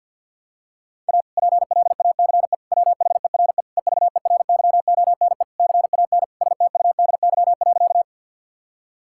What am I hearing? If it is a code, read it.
Morse code, heard as a keyed tone, about 50 wpm, 700 Hz. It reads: A PLACE WHERE EVERYONE CAN STUDY.
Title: Sentences from Top 1000 Words - Review Part 5 - 50wpm